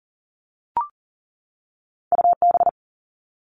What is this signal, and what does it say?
Morse code, heard as a keyed tone, about 39 wpm, 700 Hz. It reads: U B
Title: Two Challenging Letter-Number Combinations 40wpm